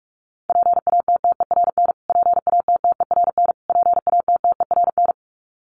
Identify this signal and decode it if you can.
Morse code, heard as a keyed tone, about 44 wpm, 700 Hz. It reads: PATTERN PATTERN PATTERN
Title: Top 500 Words Repeated 45wpm